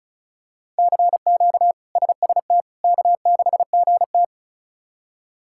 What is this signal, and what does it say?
Morse code, heard as a keyed tone, about 34 wpm, 700 Hz. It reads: CQ SST K6GT
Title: K1USN SST QSO Practice - 35wpm